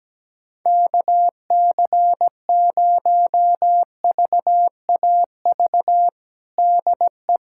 Morse code, heard as a keyed tone, about 17 wpm, 700 Hz. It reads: KC0VAV DE
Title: USA Callsign DE Callsign 17wpm